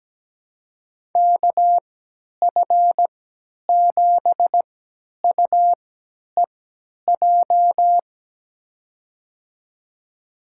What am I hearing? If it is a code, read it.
Morse code, heard as a keyed tone, about 17 wpm, spaced as for 10 wpm, 700 Hz. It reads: KF7UEJ
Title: USA Call Signs - 6 Characters - 3x Character Spacing - 17wpm